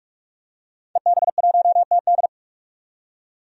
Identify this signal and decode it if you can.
Morse code, heard as a keyed tone, about 42 wpm, 700 Hz. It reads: EB1TD
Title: DXCC-100 Call Signs - 5 Characters - 45wpm